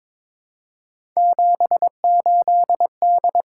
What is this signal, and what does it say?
Morse code, heard as a keyed tone, about 22 wpm, 700 Hz. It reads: 78D
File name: practice / 22wpm / Three Letter-Number Combinations - 22wpm